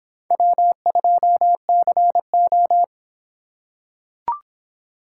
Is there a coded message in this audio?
Yes: Morse code from 0.3 to 2.8 s, heard as a keyed tone, about 26 wpm, 700 Hz. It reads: W2CO